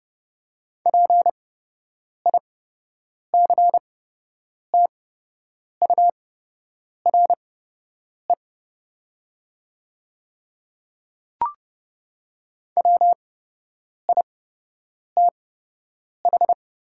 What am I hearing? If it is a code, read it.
Morse code, heard as a keyed tone, about 30 wpm, spaced as for 8 wpm, 700 Hz. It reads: PICTURE WITH
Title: 8x Character Spacing - Top 200 Words - 30wpm